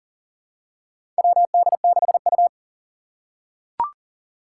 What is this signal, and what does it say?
Morse code, heard as a keyed tone, about 40 wpm, 700 Hz. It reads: WD6U
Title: USA Call Signs - 4 Characters - 2x Character Spacing - 40wpm